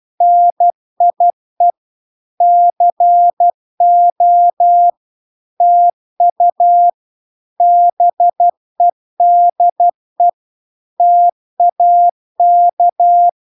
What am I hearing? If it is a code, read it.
Morse code, heard as a keyed tone, about 12 wpm, 700 Hz. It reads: NIE CO TU BEDE TAK